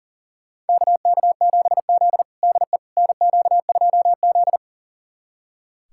Morse code, heard as a keyed tone, about 40 wpm, 700 Hz. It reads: KK7Z DE NQ2Z
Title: USA Callsign DE Callsign - 4 Characters - 40wpm